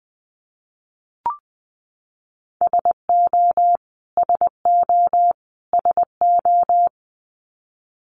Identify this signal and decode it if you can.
Morse code, heard as a keyed tone, about 20 wpm, 700 Hz. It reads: SO SO SO
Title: Top 500 Words Repeated 20wpm